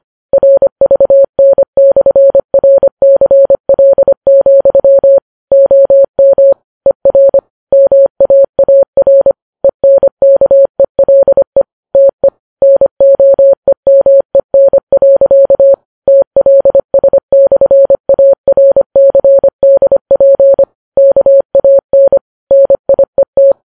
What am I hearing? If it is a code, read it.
Morse code, heard as a keyed tone, about 25 wpm, 550 Hz. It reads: RVN/RCL, OM ER MAAR ENKELE TE NOEMEN. TLS/ARCDP KAN NIET
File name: Files Web tls / Morse on9fv / morse25kps